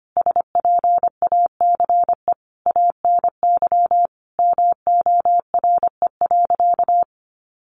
Morse code, heard as a keyed tone, about 25 wpm, 700 Hz. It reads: SPACE ANY MORE.
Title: Sentences from Top 1500 Words 25wpm